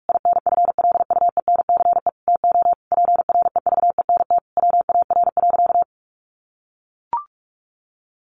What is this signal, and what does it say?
Morse code, heard as a keyed tone, about 44 wpm, 700 Hz. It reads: INFLUENCE TO PREVENT WAR.
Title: Sentences from Top 1100 Words 45wpm